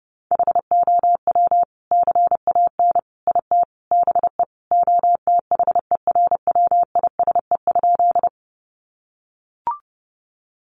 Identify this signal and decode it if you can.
Morse code, heard as a keyed tone, about 30 wpm, 700 Hz. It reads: HOW CAN IT BE OTHERWISE?
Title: Sentences from Top 1200 Words 30wpm